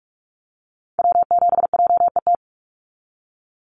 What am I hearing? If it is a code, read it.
Morse code, heard as a keyed tone, about 44 wpm, 700 Hz. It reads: W7JET
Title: NA SOTA Activator Callsigns 45wpm